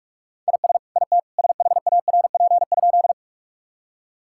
Morse code, heard as a keyed tone, about 43 wpm, 700 Hz. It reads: IS IT SHARP?
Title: Sentences from Top 1600 Words 45wpm